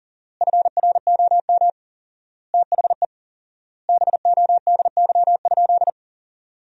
Morse code, heard as a keyed tone, about 40 wpm, 700 Hz. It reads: FROM THE BODY?
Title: 4x Word-Spacing - Sentences from Top 400 Words - 40wpm